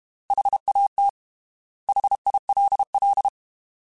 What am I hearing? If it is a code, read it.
Morse code, heard as a keyed tone, about 32 wpm, 800 Hz. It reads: HAT HILL
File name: Alphabetical / Alphabetical mp3s / Hs @30